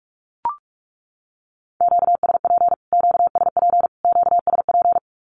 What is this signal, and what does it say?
Morse code, heard as a keyed tone, about 44 wpm, 700 Hz. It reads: QSP QSP QSP
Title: Q-Codes Repeated 3x 45wpm